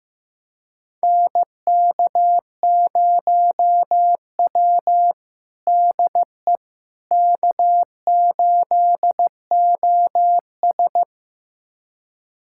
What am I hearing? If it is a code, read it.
Morse code, heard as a keyed tone, about 15 wpm, 700 Hz. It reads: NK0W DE K8OS
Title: USA Callsign DE Callsign - 4 Characters - 15wpm